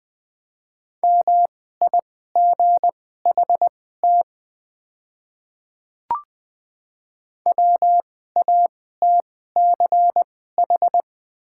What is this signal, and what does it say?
Morse code, heard as a keyed tone, about 20 wpm, spaced as for 15 wpm, 700 Hz. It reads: MIGHT WATCH
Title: 2x Character Spacing - Top 500 Words - 20wpm